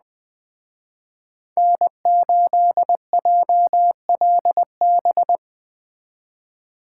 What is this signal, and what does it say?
Morse code, heard as a keyed tone, about 20 wpm, 700 Hz. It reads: N8JLB
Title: Speed Racing - USA 1x3 Call Signs 17wpm